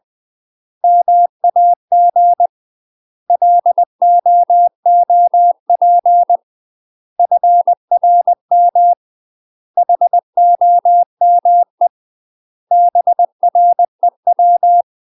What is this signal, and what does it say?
Morse code, heard as a keyed tone, about 20 wpm, 700 Hz. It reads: MAG LOOP FRM HOME BREW
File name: UK MASTER QSOs TRAINING/20WPM